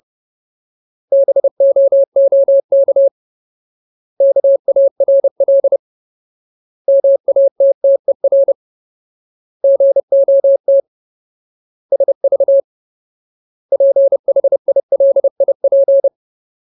Morse code, heard as a keyed tone, about 30 wpm, 550 Hz. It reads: BOOK KARL MATTER GOT SV PHILIP
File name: Ultimate Word List / 30WPM Ultimate